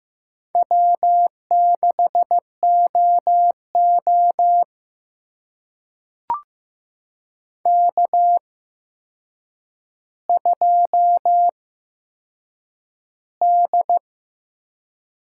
Morse code, heard as a keyed tone, about 15 wpm, spaced as for 7 wpm, 700 Hz. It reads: W6OO K2D
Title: USA Call Signs - 4 Characters - 8x Character Spacing - 15wpm